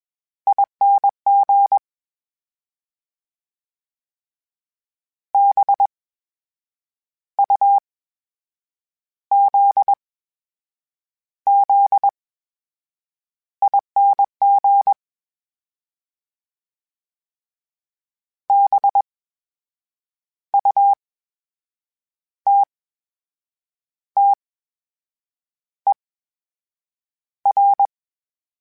Morse code, heard as a keyed tone, about 21 wpm, spaced as for 7 wpm, 800 Hz. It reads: ING BUZZING BUTTER
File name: Special INGs-mp3s / Special INGs-mp3s / VerySlowToFaster Ing @20